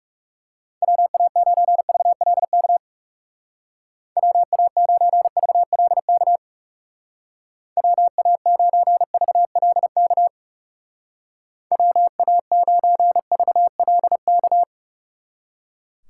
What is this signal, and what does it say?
Morse code, heard as a keyed tone, about 38 wpm, 700 Hz. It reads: WA9VLK WA9VLK WA9VLK WA9VLK